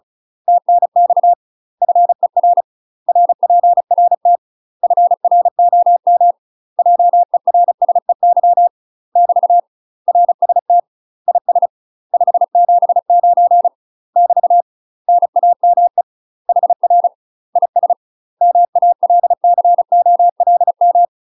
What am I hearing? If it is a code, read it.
Morse code, heard as a keyed tone, about 34 wpm, 700 Hz. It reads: TNX FER RPRT FROM JERSEY <BT> RST IS 579 <BT> NAME HR IS MALCOLM
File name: UK MASTER QSOs TRAINING/35WPM